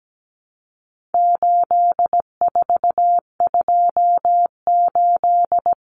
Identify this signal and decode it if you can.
Morse code, heard as a keyed tone, about 17 wpm, 700 Hz. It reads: 8428
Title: Numbers - 4-Digit 17wpm